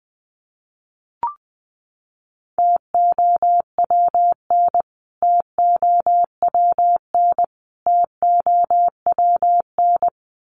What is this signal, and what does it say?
Morse code, heard as a keyed tone, about 20 wpm, 700 Hz. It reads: TOWN TOWN TOWN
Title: Top 500 Words Repeated 20wpm